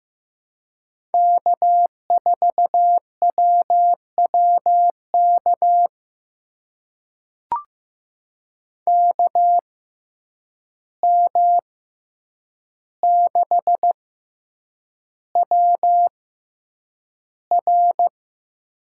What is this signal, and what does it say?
Morse code, heard as a keyed tone, about 15 wpm, spaced as for 8 wpm, 700 Hz. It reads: K4WWK KM6WR
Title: USA Call Signs - 5 Characters - 6x Character Spacing - 15wpm